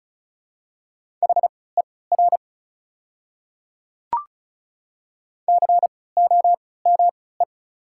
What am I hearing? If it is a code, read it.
Morse code, heard as a keyed tone, about 36 wpm, spaced as for 20 wpm, 690 Hz. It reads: HER COME
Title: 3x Character Spacing - Top 100 Words - 35wpm